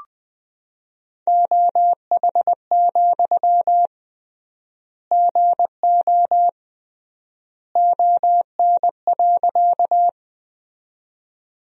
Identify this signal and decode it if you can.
Morse code, heard as a keyed tone, about 20 wpm, 700 Hz. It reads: OH, GO ON.
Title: 3x Word-Spacing - Sentences from Top 500 Words - 20wpm